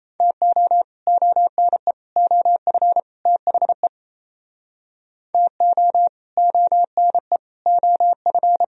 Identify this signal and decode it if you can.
Morse code, heard as a keyed tone, about 38 wpm, 700 Hz. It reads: TO ONE OF THE TO ONE OF